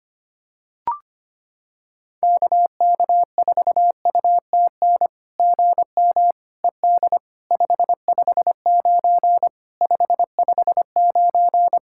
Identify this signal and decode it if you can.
Morse code, heard as a keyed tone, about 25 wpm, 700 Hz. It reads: KK4UTN GM ED 559 559
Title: SOTA QSO Practice 25wpm